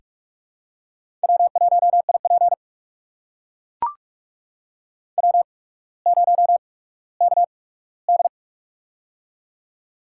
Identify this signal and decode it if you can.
Morse code, heard as a keyed tone, about 44 wpm, spaced as for 19 wpm, 700 Hz. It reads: W1IP W0KD
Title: USA Call Signs - 4 Characters - 8x Character Spacing - 45wpm